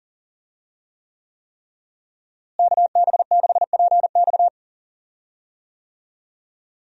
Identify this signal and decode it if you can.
Morse code, heard as a keyed tone, about 38 wpm, 700 Hz. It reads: KB6PX
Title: USA Call Signs - 5 Characters - Repeated 3x - 40wpm